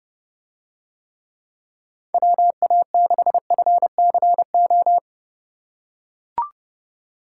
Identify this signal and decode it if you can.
Morse code, heard as a keyed tone, about 30 wpm, 700 Hz. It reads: WA6FCO